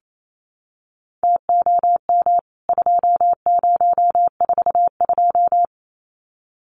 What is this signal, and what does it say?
Morse code, heard as a keyed tone, about 28 wpm, 700 Hz. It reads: TOM 2042
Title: CWT Exchange 28wpm